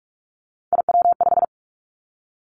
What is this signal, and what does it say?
Morse code, heard as a keyed tone, about 46 wpm, 700 Hz. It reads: IW5
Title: Three Challenging Letter-Number Combinations 45wpm